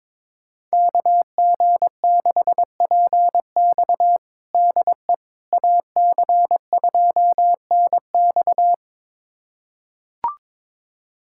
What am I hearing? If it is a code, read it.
Morse code, heard as a keyed tone, about 22 wpm, 700 Hz. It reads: KG6PX DE AC2NX
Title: USA Callsign DE Callsign - 5 Characters - 22wpm